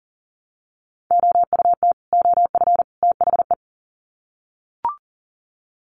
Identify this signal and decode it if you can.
Morse code, heard as a keyed tone, about 40 wpm, 700 Hz. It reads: OUT OF THE
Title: Sets of 3 Words 40wpm